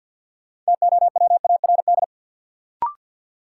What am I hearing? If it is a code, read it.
Morse code, heard as a keyed tone, about 52 wpm, 700 Hz. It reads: TOWARD
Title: Sets of 5 Words - New Word Drill - 50wpm